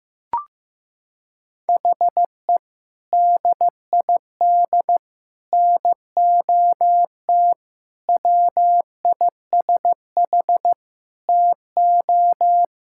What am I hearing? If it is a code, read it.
Morse code, heard as a keyed tone, about 15 wpm, 700 Hz. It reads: HE DID NOT WISH TO
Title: Sentences from Top 1600 Words 15wpm